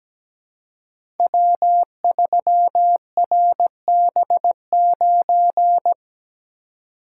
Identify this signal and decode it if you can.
Morse code, heard as a keyed tone, about 17 wpm, 700 Hz. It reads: W3RB9
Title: Code Groups - 5 Letter:Numbers - 17wpm